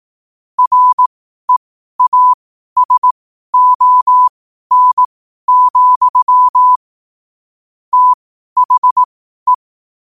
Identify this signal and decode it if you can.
Morse code, heard as a keyed tone, about 18 wpm, spaced as for 12 wpm, 1000 Hz. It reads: REASON, THE